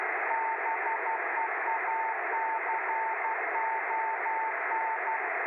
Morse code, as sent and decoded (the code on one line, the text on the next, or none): -... .- -.-. --- -.
BEACON